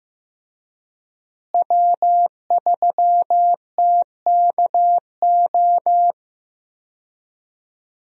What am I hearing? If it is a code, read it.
Morse code, heard as a keyed tone, about 15 wpm, 700 Hz. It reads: W3TKO
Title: USA Call Signs - 5 Characters - Repeated 3x - 15wpm